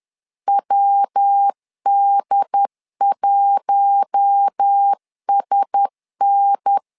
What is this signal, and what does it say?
Morse code, heard as a keyed tone, about 10 wpm, 800 Hz. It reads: WD1SN
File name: LEZIONI / 4 Q1S 10wpm 2133